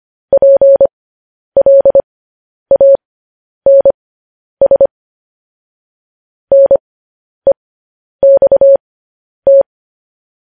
Morse code, heard as a keyed tone, about 25 wpm, spaced as for 10 wpm, 550 Hz. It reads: PLANS NEXT
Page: News Headlines in Morse Code at 10 WPM